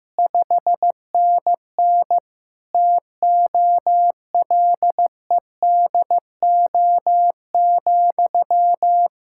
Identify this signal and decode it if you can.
Morse code, heard as a keyed tone, about 15 wpm, 700 Hz. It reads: 5NN TOLEDO,
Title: Practice QSO - Exchange 1 - Beginning - 15wpm